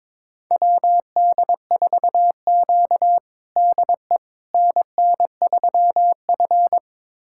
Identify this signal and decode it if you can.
Morse code, heard as a keyed tone, about 21 wpm, 700 Hz. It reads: WD4Q DE NN3F
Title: USA Callsign DE Callsign - 4 Characters - 22wpm